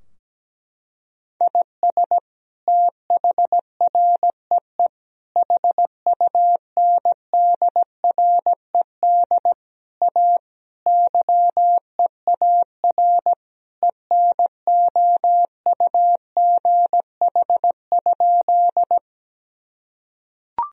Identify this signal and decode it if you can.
Morse code, heard as a keyed tone, about 17 wpm, 700 Hz. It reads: IS THREE HUNDRED A YEAR ENOUGH?
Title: Sentences from Top 1000 Words - Review Part 2 - 17wpm